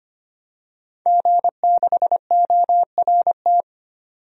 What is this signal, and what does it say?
Morse code, heard as a keyed tone, about 25 wpm, 700 Hz. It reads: G6ORT
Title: DXCC-100 Call Signs - 5 Characters - 25wpm